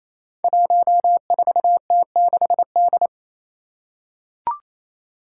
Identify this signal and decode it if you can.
Morse code, heard as a keyed tone, about 28 wpm, 700 Hz. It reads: 14T6D
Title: Five Challenging Letter-Number Combinations 28wpm